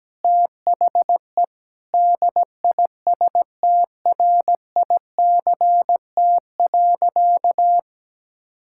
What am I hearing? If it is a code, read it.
Morse code, heard as a keyed tone, about 17 wpm, 700 Hz. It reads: THE DISTRICT.
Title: Sentences from Top 1600 Words 17wpm